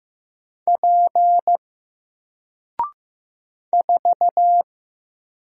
Morse code, heard as a keyed tone, about 15 wpm, 700 Hz. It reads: P 4
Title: Single Character (Letter:Number:Punctuation:Prosign) 15wpm